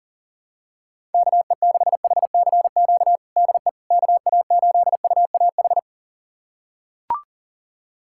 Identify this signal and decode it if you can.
Morse code, heard as a keyed tone, about 37 wpm, 700 Hz. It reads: KE6HCQ DE KA8UAH
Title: USA Callsign DE Callsign 40wpm